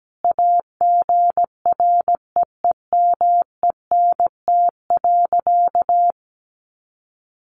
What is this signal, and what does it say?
Morse code, heard as a keyed tone, about 17 wpm, 700 Hz. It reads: AGREEMENT.
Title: Sentences from Top 1500 Words 17wpm